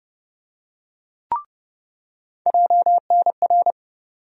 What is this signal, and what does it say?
Morse code, heard as a keyed tone, about 30 wpm, 700 Hz. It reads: JNR
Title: USA Call Sign Postfixes - 3 Letters 30wpm